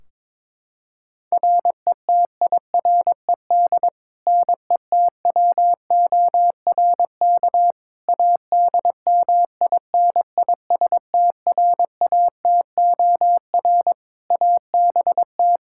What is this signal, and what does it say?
Morse code, heard as a keyed tone, about 22 wpm, 700 Hz. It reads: RETIRED NETWORK ADMINISTRATOR ABT
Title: Practice QSO - Exchange 3 - More Info - 22wpm